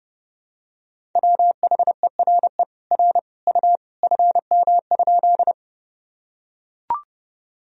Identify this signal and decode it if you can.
Morse code, heard as a keyed tone, about 32 wpm, 700 Hz. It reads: WHERE R U FM?
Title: Sentences from Top 200 Words and Abbreviations 30wpm